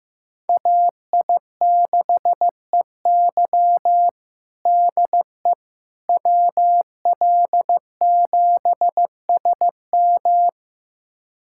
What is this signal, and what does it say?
Morse code, heard as a keyed tone, about 15 wpm, 700 Hz. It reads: AI6EY DE WL7SM
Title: USA Callsign DE Callsign - 5 Characters - 15wpm